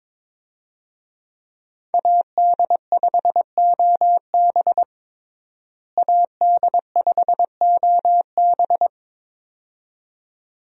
Morse code, heard as a keyed tone, about 22 wpm, 700 Hz. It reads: AD5OB AD5OB
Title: USA Call Signs - 5 Characters - Repeated 2x - 22wpm